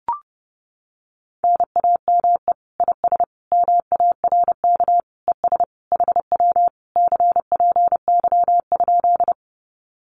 Morse code, heard as a keyed tone, about 30 wpm, 700 Hz. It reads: NAME IS MARK ES HW CPY?
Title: Common QSO Phrases Encore 30wpm